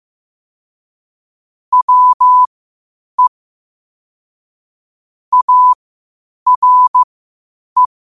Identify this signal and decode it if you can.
Morse code, heard as a keyed tone, about 15 wpm, spaced as for 8 wpm, 1000 Hz. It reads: WE ARE
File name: qotdfiles / 8wpm